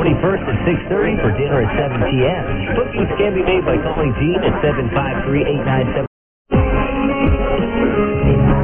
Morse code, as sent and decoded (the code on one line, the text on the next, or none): none